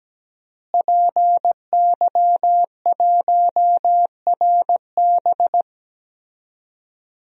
Morse code, heard as a keyed tone, about 17 wpm, 700 Hz. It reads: PY1RB
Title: DXCC-100 Call Signs - 5 Characters - Repeated 2x - 17wpm